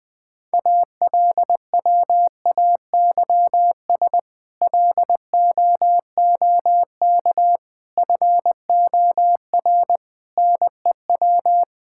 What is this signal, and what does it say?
Morse code, heard as a keyed tone, about 20 wpm, 700 Hz. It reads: ALWAYS LOOK FOR NEW
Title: Sentences with 6-Word Phrases - 20wpm